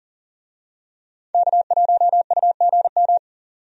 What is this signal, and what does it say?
Morse code, heard as a keyed tone, about 40 wpm, 700 Hz. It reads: K1UGM
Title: USA Call Signs - 5 Characters - 40wpm